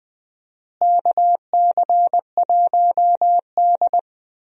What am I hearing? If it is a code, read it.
Morse code, heard as a keyed tone, about 20 wpm, 700 Hz. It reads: KC1D
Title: USA Call Signs - 4 Characters - Repeated 2x - 20wpm